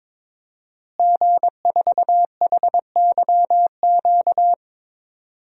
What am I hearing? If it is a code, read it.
Morse code, heard as a keyed tone, about 22 wpm, 700 Hz. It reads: G4HYQ
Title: DXCC-100 Call Signs - 5 Characters - Repeated 2x - 22wpm